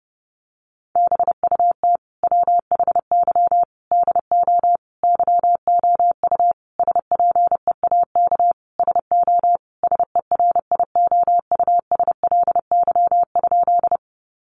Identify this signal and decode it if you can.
Morse code, heard as a keyed tone, about 28 wpm, 700 Hz. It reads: BUT WHY DO YOU SPEAK SO SERIOUSLY?
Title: Sentences from Top 1200 Words 30wpm